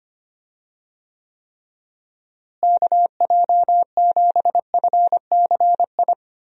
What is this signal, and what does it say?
Morse code, heard as a keyed tone, about 25 wpm, 700 Hz. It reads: KJ7FCI
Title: USA Call Signs - 6 Characters - Repeated 3x - 25wpm